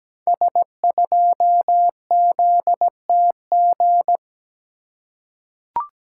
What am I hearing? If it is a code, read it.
Morse code, heard as a keyed tone, about 17 wpm, 700 Hz. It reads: S2ZTG
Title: Code Groups - 5 Letter:Numbers - 17wpm